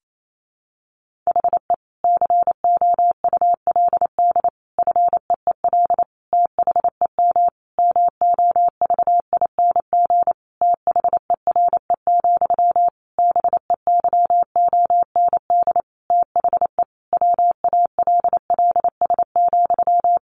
Morse code, heard as a keyed tone, about 28 wpm, 700 Hz. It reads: HE COULD FEEL THEM MOVING THERE, BEYOND THE WALLS,
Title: A Sound Of Thunder 28wpm